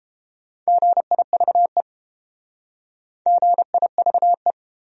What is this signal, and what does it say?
Morse code, heard as a keyed tone, about 31 wpm, 700 Hz. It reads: GIVE GIVE